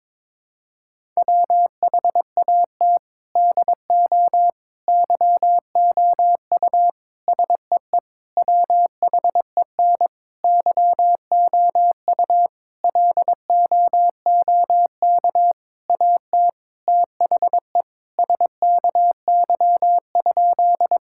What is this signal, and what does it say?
Morse code, heard as a keyed tone, about 22 wpm, 700 Hz. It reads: WHAT DO YOU SEE WHEN YOU LOOK AT THE SKY?
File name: practice / 22wpm / Sentences with 4-Word Phrases - 22wpm